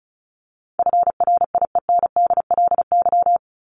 Morse code, heard as a keyed tone, about 34 wpm, 700 Hz. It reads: FRIENDLY
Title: Top 1751-2000 Words 35wpm